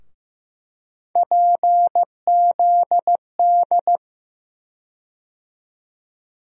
Morse code, heard as a keyed tone, about 15 wpm, 700 Hz. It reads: PZD
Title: USA Call Sign Postfixes - 3 Letters 15wpm